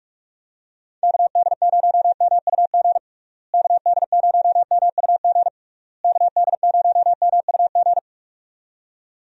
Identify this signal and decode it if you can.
Morse code, heard as a keyed tone, about 46 wpm, 700 Hz. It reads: KD0MUG KD0MUG KD0MUG
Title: USA Call Signs - 6 Characters - Repeated 3x - 45wpm